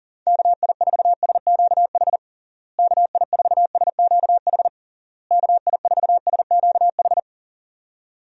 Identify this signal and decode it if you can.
Morse code, heard as a keyed tone, about 40 wpm, 700 Hz. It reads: KI4SQH KI4SQH KI4SQH
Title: USA Call Signs - 6 Characters - Repeated 3x - 40wpm